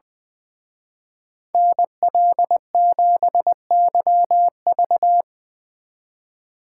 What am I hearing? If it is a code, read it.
Morse code, heard as a keyed tone, about 22 wpm, 700 Hz. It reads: NL7YV
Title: Speed Racing - USA 2x2 Call Signs 20wpm